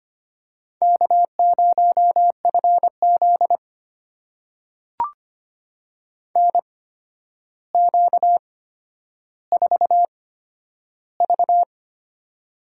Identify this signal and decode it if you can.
Morse code, heard as a keyed tone, about 25 wpm, spaced as for 11 wpm, 700 Hz. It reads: K0FZ NQ4V
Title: USA Call Signs - 4 Characters - 8x Character Spacing - 25wpm